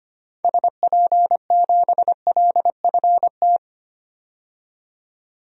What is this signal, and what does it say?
Morse code, heard as a keyed tone, about 25 wpm, 690 Hz. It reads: SP7LFT
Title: DXCC-100 Call Signs - 6 Characters - 25wpm